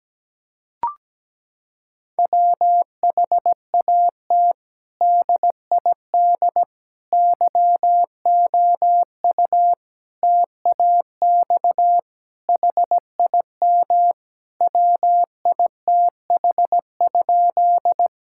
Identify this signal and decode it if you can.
Morse code, heard as a keyed tone, about 17 wpm, 700 Hz. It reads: WHAT DID YOU TAX HIM WITH?
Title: Sentences from Top 1000 Words - Review Part 5 - 17wpm